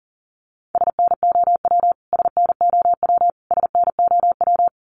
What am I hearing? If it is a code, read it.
Morse code, heard as a keyed tone, about 40 wpm, 700 Hz. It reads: SNOW SNOW SNOW
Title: Top 500 Words Repeated 40wpm